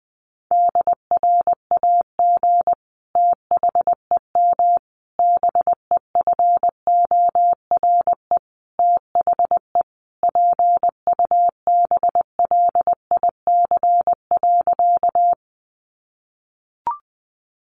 Morse code, heard as a keyed tone, about 20 wpm, 700 Hz. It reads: DRAG THEM BEFORE THE PUBLIC.
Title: Sentences from Top 1500 Words 20wpm